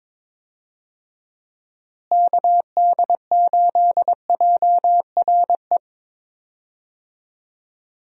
Morse code, heard as a keyed tone, about 22 wpm, 700 Hz. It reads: KD8JRE
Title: USA Call Signs - 6 Characters - Repeated 3x - 22wpm